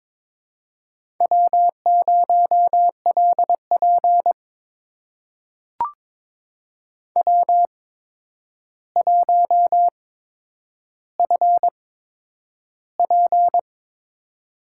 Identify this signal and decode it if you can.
Morse code, heard as a keyed tone, about 22 wpm, spaced as for 9 wpm, 700 Hz. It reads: W0LP W1FP
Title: USA Call Signs - 4 Characters - 8x Character Spacing - 22wpm